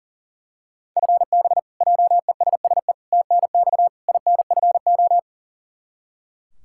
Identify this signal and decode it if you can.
Morse code, heard as a keyed tone, about 37 wpm, 700 Hz. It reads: FB JESSE TNX INFO
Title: K1USN SST QSO Practice 40wpm